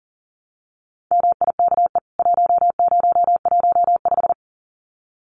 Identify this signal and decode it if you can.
Morse code, heard as a keyed tone, about 40 wpm, 700 Hz. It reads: MIKE 1015
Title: CWT Exchange 40wpm